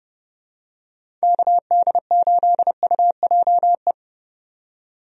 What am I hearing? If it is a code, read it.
Morse code, heard as a keyed tone, about 30 wpm, 700 Hz. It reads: KD8UJE
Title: USA Call Signs - 6 Characters - 30wpm